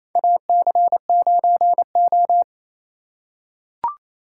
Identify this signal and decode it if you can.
Morse code, heard as a keyed tone, about 28 wpm, 700 Hz. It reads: AC9O
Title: USA Call Signs - 4 Characters - 2x Character Spacing - 28wpm